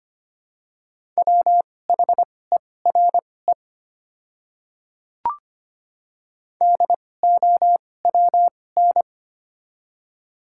Morse code, heard as a keyed tone, about 25 wpm, spaced as for 18 wpm, 700 Hz. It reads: WHERE DOWN
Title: 2x Character Spacing - Top 200 Words - 25wpm